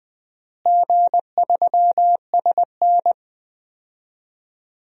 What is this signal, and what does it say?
Morse code, heard as a keyed tone, about 20 wpm, 700 Hz. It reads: G3SN
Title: DXCC-100 Call Signs - 4 Characters - Repeated 2x - 20wpm